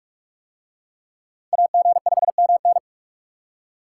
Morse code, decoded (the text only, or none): AG5MN